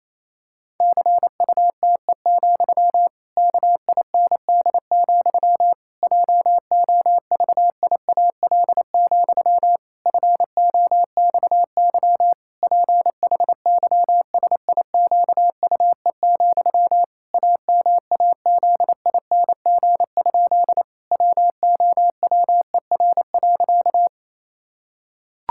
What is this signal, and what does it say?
Morse code, heard as a keyed tone, about 28 wpm, 700 Hz. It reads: CUTE, KIND, JOVIAL, FOXY PHYSIQUE, AMAZING? WOWER.
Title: Pangram Challenge v2 - 28wpm